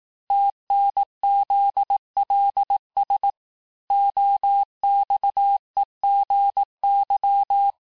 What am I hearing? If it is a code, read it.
Morse code, heard as a keyed tone, about 18 wpm, 800 Hz. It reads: TNZLS OXEGY
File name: cours-du-lundi files / cours-1588-texte-chiffre-vitesse-17MPM